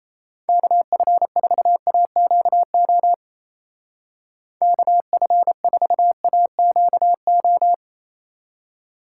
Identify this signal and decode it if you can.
Morse code, heard as a keyed tone, about 34 wpm, 700 Hz. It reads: KF4AQO KF4AQO
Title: Speed Racing - USA 2x3 Call Signs 28wpm